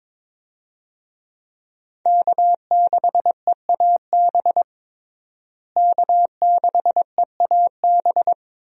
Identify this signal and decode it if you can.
Morse code, heard as a keyed tone, about 21 wpm, 700 Hz. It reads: K6EAB K6EAB
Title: USA Call Signs - 5 Characters - Repeated 2x - 22wpm